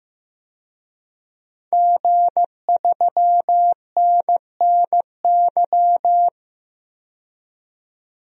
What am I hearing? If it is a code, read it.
Morse code, heard as a keyed tone, about 15 wpm, 700 Hz. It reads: G3NNY
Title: DXCC-100 Call Signs - 5 Characters - 15wpm